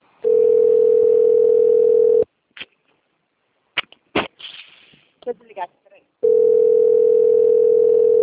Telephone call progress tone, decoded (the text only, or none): ringback tone